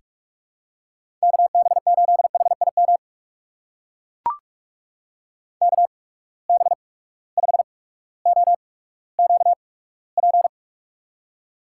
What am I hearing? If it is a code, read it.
Morse code, heard as a keyed tone, about 44 wpm, spaced as for 19 wpm, 700 Hz. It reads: KB8HIM KB5OQP
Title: USA Call Signs - 6 Characters - 8x Character Spacing - 45wpm